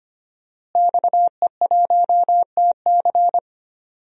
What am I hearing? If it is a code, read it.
Morse code, heard as a keyed tone, about 25 wpm, 700 Hz. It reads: XE1TC
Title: DXCC-100 Call Signs - 5 Characters - Repeated 3x - 25wpm